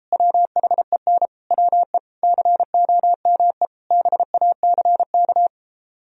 Morse code, heard as a keyed tone, about 34 wpm, 700 Hz. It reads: WHEN WE COME BACK
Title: Speed-Racing - Sets of 5 Words 28wpm